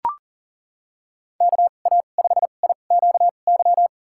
Morse code, heard as a keyed tone, about 37 wpm, spaced as for 28 wpm, 700 Hz. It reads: KA5IQY
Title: USA Call Signs - 2x Character Spacing - 40wpm